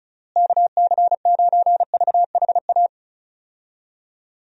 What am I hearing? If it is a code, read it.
Morse code, heard as a keyed tone, about 34 wpm, 700 Hz. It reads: KC9VHA
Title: USA Call Signs - 6 Characters - Repeated 3x - 35wpm